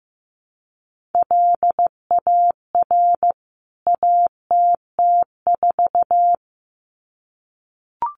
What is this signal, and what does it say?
Morse code, heard as a keyed tone, about 15 wpm, 700 Hz. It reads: LAR ATT4
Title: CWT Exchange 15wpm